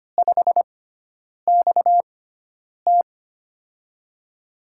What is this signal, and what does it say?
Morse code, heard as a keyed tone, about 25 wpm, spaced as for 9 wpm, 700 Hz. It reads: 5XT
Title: USA Call Signs - 4 Characters - 6x Character Spacing - 25wpm